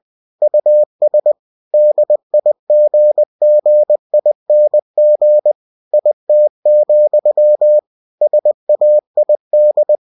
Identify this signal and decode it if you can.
Morse code, heard as a keyed tone, about 20 wpm, 600 Hz. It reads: US DIGGING IT, SAID
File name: cw-books / winnie-the-pooh / winnie-20wpm-page-0036